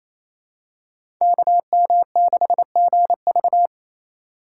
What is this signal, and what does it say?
Morse code, heard as a keyed tone, about 28 wpm, 700 Hz. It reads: KM6GV
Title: USA Call Signs - 2x Character Spacing - 28wpm